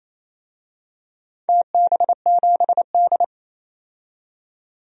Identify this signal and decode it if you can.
Morse code, heard as a keyed tone, about 27 wpm, 700 Hz. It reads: TB7D
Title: DXCC-100 Call Signs - 4 Characters - 28wpm